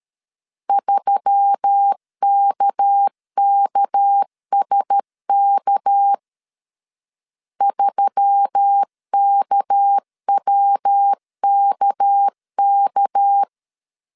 Morse code, heard as a keyed tone, about 13 wpm, 800 Hz. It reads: 3KKSK 3KWKK
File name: LEZIONI / 6 PKX0 / 6 PK 12wpm 2006